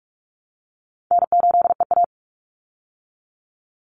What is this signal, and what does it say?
Morse code, heard as a keyed tone, about 44 wpm, 700 Hz. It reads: N8EA
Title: Call Signs Repeated 2x - 45wpm